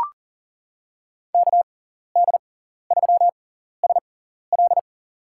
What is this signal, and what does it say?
Morse code, heard as a keyed tone, about 39 wpm, spaced as for 14 wpm, 700 Hz. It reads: KD3SL
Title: USA Call Signs - 5 Characters - 6x Character Spacing - 40wpm